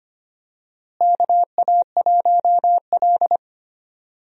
Morse code, heard as a keyed tone, about 25 wpm, 700 Hz. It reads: KA1L
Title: USA Call Signs - 4 Characters - Repeated 2x - 25wpm